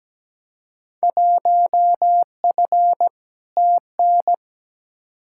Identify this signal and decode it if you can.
Morse code, heard as a keyed tone, about 17 wpm, 700 Hz. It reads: 1F TN
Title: Field Day Exchanges - 17wpm